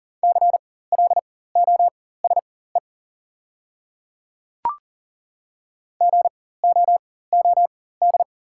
Morse code, heard as a keyed tone, about 40 wpm, spaced as for 19 wpm, 700 Hz. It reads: CLOSE GOOD
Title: 4x Character Spacing - Top 300 Words - 40wpm